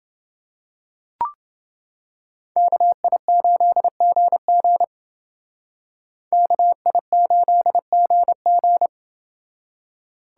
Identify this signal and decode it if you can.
Morse code, heard as a keyed tone, about 28 wpm, 700 Hz. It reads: KI8GG KI8GG